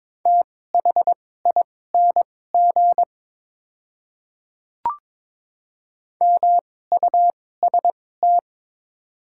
Morse code, heard as a keyed tone, about 22 wpm, spaced as for 16 wpm, 700 Hz. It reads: THING MUST
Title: 2x Character Spacing - Top 200 Words - 22wpm